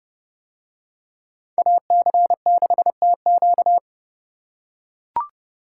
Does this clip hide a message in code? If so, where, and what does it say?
Yes: Morse code from 1.6 to 3.8 s, heard as a keyed tone, about 29 wpm, 700 Hz. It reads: AC6TQ